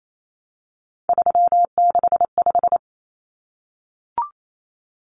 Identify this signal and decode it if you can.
Morse code, heard as a keyed tone, about 28 wpm, 700 Hz. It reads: 365
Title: Common Cut Numbers - 3-Digit 28wpm